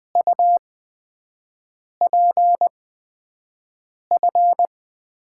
Morse code, heard as a keyed tone, about 20 wpm, spaced as for 5 wpm, 700 Hz. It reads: UPF
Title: USA Call Signs - 6 Characters - 8x Character Spacing - 20wpm